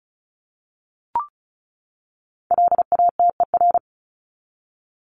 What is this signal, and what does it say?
Morse code, heard as a keyed tone, about 34 wpm, 700 Hz. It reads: LATER
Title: Top 501-750 Words 35wpm